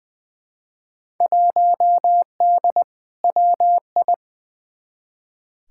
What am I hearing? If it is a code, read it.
Morse code, heard as a keyed tone, about 20 wpm, 700 Hz. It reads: 1D WI
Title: Field Day Exchanges - 20wpm